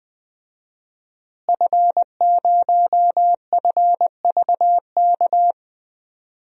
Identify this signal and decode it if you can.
Morse code, heard as a keyed tone, about 21 wpm, 700 Hz. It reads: F0FVK
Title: DXCC-100 Call Signs - 5 Characters - Repeated 3x - 20wpm